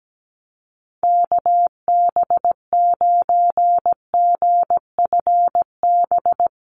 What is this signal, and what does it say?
Morse code, heard as a keyed tone, about 17 wpm, 700 Hz. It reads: KB9GFB
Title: USA Call Signs Repeated 2x 17wpm